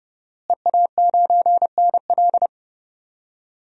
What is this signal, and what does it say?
Morse code, heard as a keyed tone, about 30 wpm, 700 Hz. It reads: EA9NL
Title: DXCC-100 Call Signs - 5 Characters - Repeated 3x - 30wpm